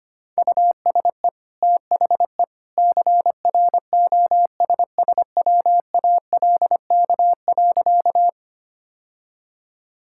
Morse code, heard as a keyed tone, about 25 wpm, 700 Hz. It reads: USE THE CROSSWALK.
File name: Easy Sentences with Hints - 25wpm